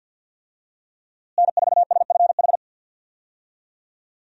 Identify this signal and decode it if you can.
Morse code, heard as a keyed tone, about 52 wpm, 700 Hz. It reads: N4SFH
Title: USA Call Signs - 5 Characters - Repeated 2x - 50wpm